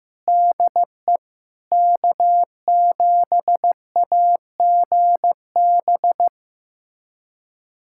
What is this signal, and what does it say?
Morse code, heard as a keyed tone, about 15 wpm, 700 Hz. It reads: DE K7AGB
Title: USA Callsign DE Callsign 15wpm